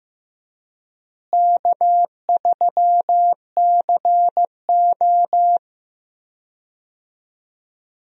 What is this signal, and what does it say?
Morse code, heard as a keyed tone, about 15 wpm, 700 Hz. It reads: K3CO
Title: USA Call Signs - 4 Characters - Repeated 2x - 15wpm